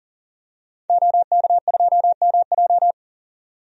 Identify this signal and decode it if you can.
Morse code, heard as a keyed tone, about 40 wpm, 700 Hz. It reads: OK2MJ
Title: DXCC-100 Call Signs - 5 Characters - 40wpm